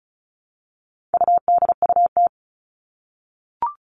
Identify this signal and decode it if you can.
Morse code, heard as a keyed tone, about 34 wpm, 700 Hz. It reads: UDUT
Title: All Cut Numbers - 4-Digit 35wpm